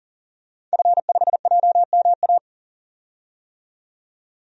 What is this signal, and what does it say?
Morse code, heard as a keyed tone, about 40 wpm, 700 Hz. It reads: F5JMA
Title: DXCC-100 Call Signs - 5 Characters - 40wpm